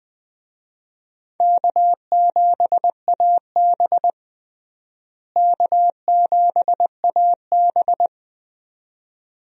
Morse code, heard as a keyed tone, about 20 wpm, 700 Hz. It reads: K7AB K7AB
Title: USA Call Signs - 4 Characters - Repeated 2x - 20wpm